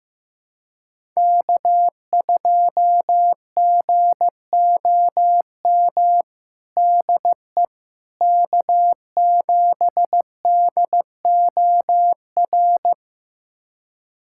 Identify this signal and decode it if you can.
Morse code, heard as a keyed tone, about 15 wpm, 700 Hz. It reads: K2GOM DE K7DOR
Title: USA Callsign DE Callsign - 5 Characters - 15wpm